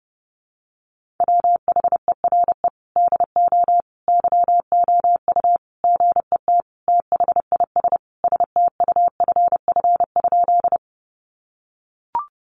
Morse code, heard as a keyed tone, about 30 wpm, 700 Hz. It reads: WHERE DO YOU GET THIS STUFF?
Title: Sentences from Top 1400 Words 30wpm